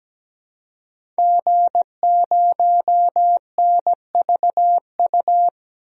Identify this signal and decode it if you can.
Morse code, heard as a keyed tone, about 17 wpm, 700 Hz. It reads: G0NVU
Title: DXCC-100 Call Signs - 5 Characters - 17wpm